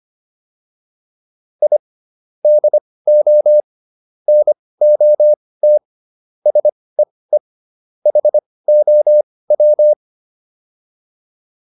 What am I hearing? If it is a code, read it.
Morse code, heard as a keyed tone, about 25 wpm, spaced as for 18 wpm, 600 Hz. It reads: I DO NOT SEE HOW